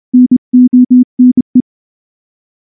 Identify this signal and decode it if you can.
Morse code, heard as a keyed tone, about 24 wpm, 250 Hz. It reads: NONE
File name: morse / morse-training-25WPM